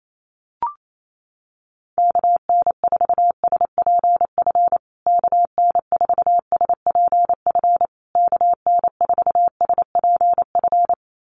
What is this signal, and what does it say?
Morse code, heard as a keyed tone, about 28 wpm, 700 Hz. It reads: KN4SPF KN4SPF KN4SPF
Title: USA Call Signs Repeated 3x 28wpm